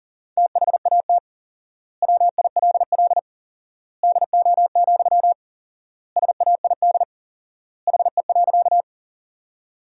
Morse code, heard as a keyed tone, about 40 wpm, 700 Hz. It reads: THAT WILL DO, SAID HE.
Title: 4x Word-Spacing - Sentences from Top 100 Words - 40wpm